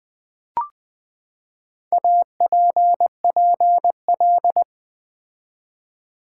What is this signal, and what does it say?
Morse code, heard as a keyed tone, about 20 wpm, 700 Hz. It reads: APPL
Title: Build A Word - 4 Letters At A Time - 10-Letter Words - 20wpm